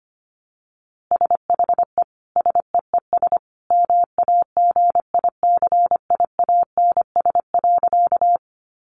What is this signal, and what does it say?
Morse code, heard as a keyed tone, about 25 wpm, 700 Hz. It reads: SHE SEES MAGICIANS.
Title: Easy Phrases 25wpm